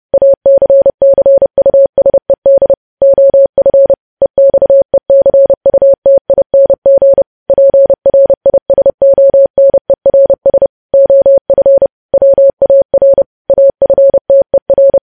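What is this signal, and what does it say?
Morse code, heard as a keyed tone, about 30 wpm, 550 Hz. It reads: ACCUSED OF EXECUTING PRISONERS OF WAR AFTER
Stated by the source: Morse code news headlines sent at 30.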